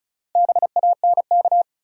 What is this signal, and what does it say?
Morse code, heard as a keyed tone, about 36 wpm, 700 Hz. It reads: BANK
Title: Sets of 5 Words - New Word Drill - 35wpm